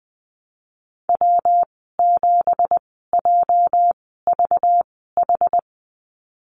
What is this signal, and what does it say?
Morse code, heard as a keyed tone, about 20 wpm, spaced as for 14 wpm, 700 Hz. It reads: W7JVH
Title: 231 Lesson 37g - Letter x - Call Signs 20wpm